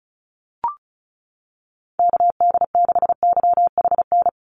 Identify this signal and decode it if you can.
Morse code, heard as a keyed tone, about 36 wpm, 700 Hz. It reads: KD6YHN
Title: USA Call Signs 35wpm